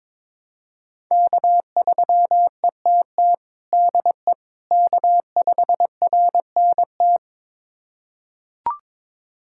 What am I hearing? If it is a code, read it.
Morse code, heard as a keyed tone, about 21 wpm, 700 Hz. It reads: K3ETT DE K5RNT
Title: USA Callsign DE Callsign - 5 Characters - 22wpm